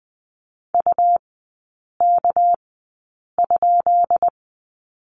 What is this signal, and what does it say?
Morse code, heard as a keyed tone, about 20 wpm, spaced as for 8 wpm, 700 Hz. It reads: UK?
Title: 212 Lesson 34e - Number 8 - Three Characters 20wpm